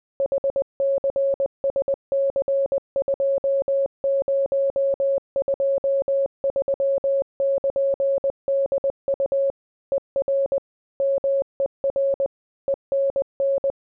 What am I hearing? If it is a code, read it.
Morse code, heard as a keyed tone, about 20 wpm, 550 Hz. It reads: HCSC2023<KN>DU ER MER ENN